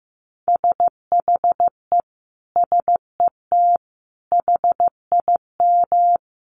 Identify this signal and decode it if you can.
Morse code, heard as a keyed tone, about 15 wpm, 700 Hz. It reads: SHE SET HIM
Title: Sentences from Top 1500 Words 15wpm